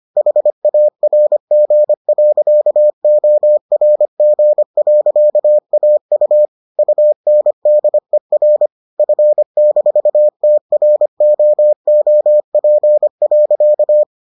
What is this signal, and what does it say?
Morse code, heard as a keyed tone, about 25 wpm, 600 Hz. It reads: HARG.ORG.AU UNDER F-TROOP.